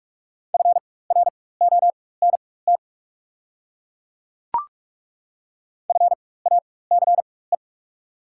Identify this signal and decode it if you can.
Morse code, heard as a keyed tone, about 42 wpm, spaced as for 20 wpm, 690 Hz. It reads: FRONT FACE
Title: 4x Character Spacing - Top 500 Words - 45wpm